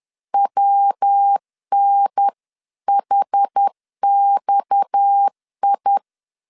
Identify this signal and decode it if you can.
Morse code, heard as a keyed tone, about 10 wpm, 800 Hz. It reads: WNHXI
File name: CW / DET / 1 6 1500C S B 10wpm 4137